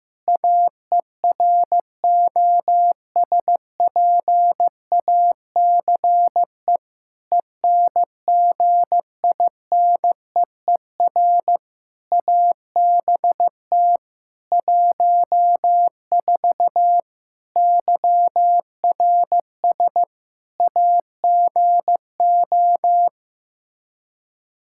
Morse code, heard as a keyed tone, about 15 wpm, 700 Hz. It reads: AEROSPACE ENGINEER ABT 14 YRS AGO
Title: K1USN SST QSO Practice 15wpm